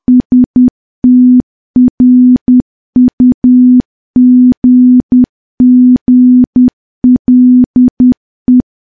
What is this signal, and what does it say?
Morse code, heard as a keyed tone, about 10 wpm, 250 Hz. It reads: STRUGGLE
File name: morse / morse-training-10WPM